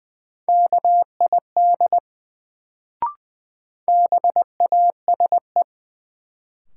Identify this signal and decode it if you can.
Morse code, heard as a keyed tone, about 20 wpm, 700 Hz. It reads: KID BASE
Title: Sentences from Top 1600 Words - New Word Drill Alt - 20wpm